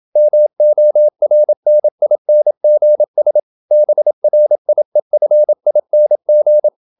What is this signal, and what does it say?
Morse code, heard as a keyed tone, about 27 wpm, 600 Hz. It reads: MORNINGS BRIEFING
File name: Never 0027wpm